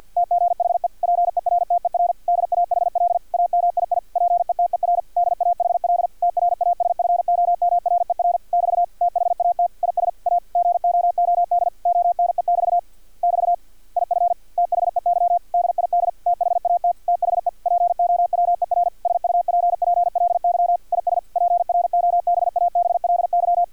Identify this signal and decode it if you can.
Morse code, heard as a keyed tone, about 50 wpm, 700 Hz. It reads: TGHE PERTER DAHL AMII PETER DAHL TRASFOMRER <BT> THAT IS A GOOD ONE<BT> <BT> IF THEY DID THAT THE POWER SUPPLY IS PROBABLY
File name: archived / CRSnet